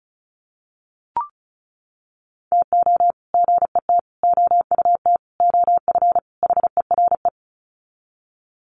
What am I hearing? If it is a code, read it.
Morse code, heard as a keyed tone, about 34 wpm, 700 Hz. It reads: TO GET OUT OF HERE
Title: Sets of 5 Words Encore 35wpm